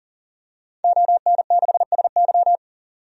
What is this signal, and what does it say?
Morse code, heard as a keyed tone, about 40 wpm, 700 Hz. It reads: ON6SY
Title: DXCC-100 Call Signs - 5 Characters - 40wpm